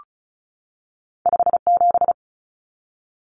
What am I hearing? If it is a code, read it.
Morse code, heard as a keyed tone, about 36 wpm, 700 Hz. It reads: 57
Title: Numbers - 2-Digit 35wpm